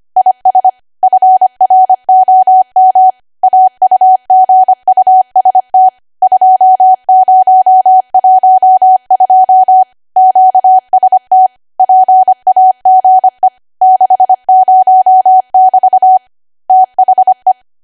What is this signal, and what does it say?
Morse code, heard as a keyed tone, about 25 wpm, 750 Hz. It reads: IS FROM AUGUST 2012 QST PAGE 60<BT> THE
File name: Archive / 25 WPM / 25WPM